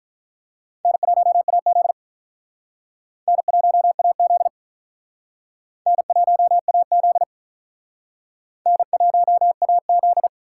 Morse code, heard as a keyed tone, about 43 wpm, 700 Hz. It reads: N1AZ N1AZ N1AZ N1AZ
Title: Speed Racing - USA 1x2 Call Signs 35wpm